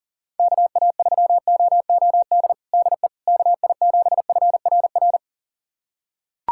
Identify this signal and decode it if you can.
Morse code, heard as a keyed tone, about 40 wpm, 700 Hz. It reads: KA3OOD DE KI7FRR
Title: USA Callsign DE Callsign 40wpm